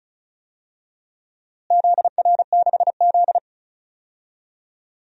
Morse code, heard as a keyed tone, about 36 wpm, 700 Hz. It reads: ZR6Z
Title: DXCC-100 Call Signs - 4 Characters - Repeated 2x - 35wpm